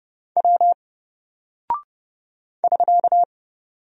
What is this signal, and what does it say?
Morse code, heard as a keyed tone, about 30 wpm, 700 Hz. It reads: W <SK>
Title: Single Character (Letter:Number:Punctuation:Prosign) 30wpm